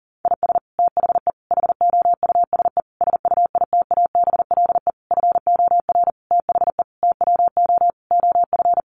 Morse code, heard as a keyed tone, about 40 wpm, 700 Hz. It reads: IS THE HOUSE SUITABLE FOR THE TWO OF
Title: Sentences from Top 1400 Words 40wpm